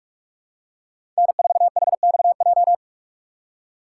Morse code, heard as a keyed tone, about 44 wpm, 700 Hz. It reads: N4HXJ
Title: USA Call Signs - 5 Characters - 4x Character Spacing - 45wpm